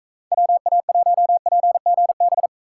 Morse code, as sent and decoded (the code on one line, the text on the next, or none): .-- .- .---- .--. --. -...
WA1PGB